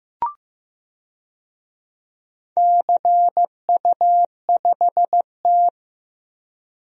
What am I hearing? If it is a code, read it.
Morse code, heard as a keyed tone, about 15 wpm, 700 Hz. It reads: CU5T
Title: DXCC-100 Call Signs - 4 Characters - Repeated 2x - 15wpm